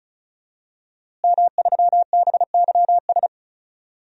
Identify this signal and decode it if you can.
Morse code, heard as a keyed tone, about 34 wpm, 700 Hz. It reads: M3BYS
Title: DXCC-100 Call Signs - 5 Characters - Repeated 2x - 35wpm